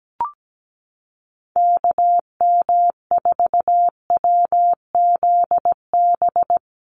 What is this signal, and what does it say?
Morse code, heard as a keyed tone, about 17 wpm, 700 Hz. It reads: KM4WZB
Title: USA Call Signs Repeated 3x 17wpm